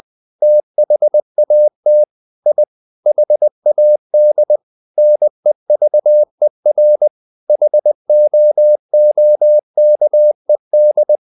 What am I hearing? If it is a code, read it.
Morse code, heard as a keyed tone, about 20 wpm, 600 Hz. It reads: THAT I HAD NEVER HOOKED
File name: the old man and the sea part 2 20wpm